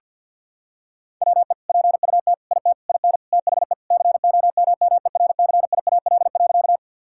Morse code, heard as a keyed tone, about 50 wpm, 700 Hz. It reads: WE PUT IT IN THE COMMERCIAL.
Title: Sentences from Top 1000 Words - Review Part 5 - 50wpm